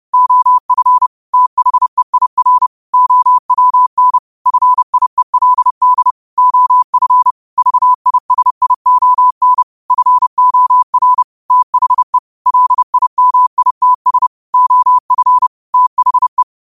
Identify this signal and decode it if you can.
Morse code, heard as a keyed tone, about 30 wpm, 1000 Hz. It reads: OF THEIR OWN FIELD OF VISION FOR THE LIMITS OF THE